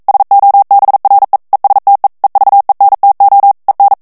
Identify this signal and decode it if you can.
Morse code, heard as a keyed tone, about 42 wpm, 800 Hz. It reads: SOBRE ESTE EVENTO EN
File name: morse mp3 / ESPAGNOL TEXTE EN CLAIR 1 VITESSE 40 MOTS MINUTES